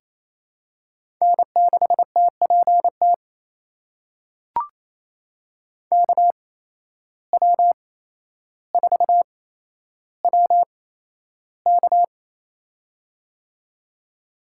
Morse code, heard as a keyed tone, about 28 wpm, spaced as for 12 wpm, 700 Hz. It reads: N6TPT KW4WK
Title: USA Call Signs - 5 Characters - 8x Character Spacing - 28wpm